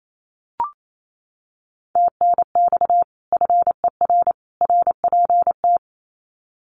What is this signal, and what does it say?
Morse code, heard as a keyed tone, about 28 wpm, 700 Hz. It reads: TNX FER RPT
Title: Common QSO Phrases 28wpm